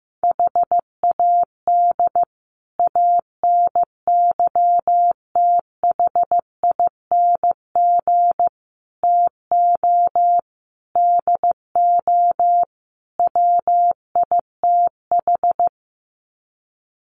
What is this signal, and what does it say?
Morse code, heard as a keyed tone, about 15 wpm, 700 Hz. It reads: HAD ANYTHING TO DO WITH
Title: Sets of 5 Words Encore 15wpm